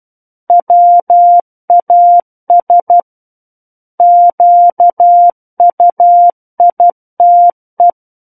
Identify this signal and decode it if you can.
Morse code, heard as a keyed tone, about 12 wpm, 700 Hz. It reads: WAS QUITE
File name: Book One / 03-Chapter 3 - 12 WPM